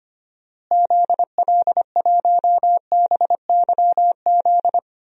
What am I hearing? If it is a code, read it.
Morse code, heard as a keyed tone, about 25 wpm, 700 Hz. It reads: ZL1BYZ
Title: DXCC-100 Call Signs - 6 Characters - Repeated 2x - 25wpm